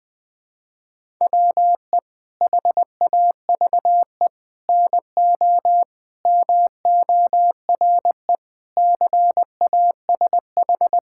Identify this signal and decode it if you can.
Morse code, heard as a keyed tone, about 21 wpm, 700 Hz. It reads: WE HAVE NO MORE CASH
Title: Sentences from Top 1000 Words - Review Part 5 - 20wpm